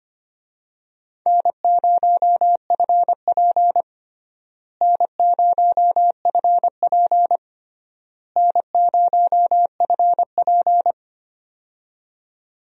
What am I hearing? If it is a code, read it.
Morse code, heard as a keyed tone, about 25 wpm, 700 Hz. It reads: N0FP N0FP N0FP
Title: USA Call Signs - 4 Characters - Repeated 3x - 25wpm